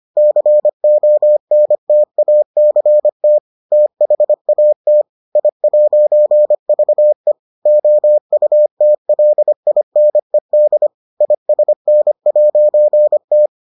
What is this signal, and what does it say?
Morse code, heard as a keyed tone, about 25 wpm, 600 Hz. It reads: CONTACT THAT I'VE OUTLINED ISN'T